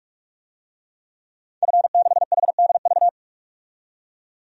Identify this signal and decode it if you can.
Morse code, heard as a keyed tone, about 46 wpm, 700 Hz. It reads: F6HDV
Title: DXCC-100 Call Signs - 5 Characters - 45wpm